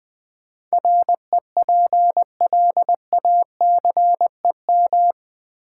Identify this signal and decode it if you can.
Morse code, heard as a keyed tone, about 20 wpm, 700 Hz. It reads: REPLACEM
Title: Build A Word - 4 Letters At A Time - 11-Letter Words - 20wpm